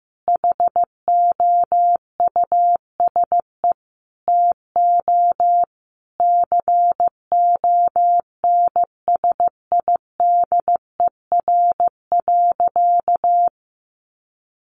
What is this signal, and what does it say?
Morse code, heard as a keyed tone, about 15 wpm, 700 Hz. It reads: HOUSE TO CONSIDER.
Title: Sentences from Top 900 Words 15wpm